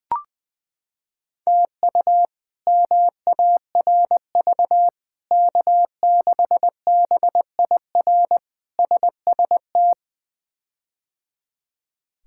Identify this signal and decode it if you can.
Morse code, heard as a keyed tone, about 20 wpm, 700 Hz. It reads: TU MARV K6BIR SST
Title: K1USN SST QSO Practice - 20wpm